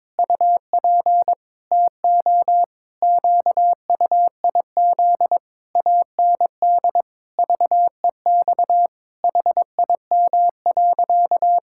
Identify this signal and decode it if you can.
Morse code, heard as a keyed tone, about 22 wpm, 700 Hz. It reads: UP TO QUIZ AND VEX HIM.
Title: Pangram Challenge v2 - 22wpm